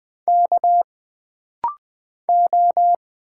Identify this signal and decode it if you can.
Morse code, heard as a keyed tone, about 20 wpm, 700 Hz. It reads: K O
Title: Single Letters - Rapid-Fire 20wpm